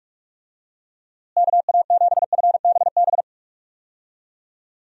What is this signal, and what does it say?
Morse code, heard as a keyed tone, about 46 wpm, 700 Hz. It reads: KA7FBB
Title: USA Call Signs - 6 Characters - Repeated 2x - 45wpm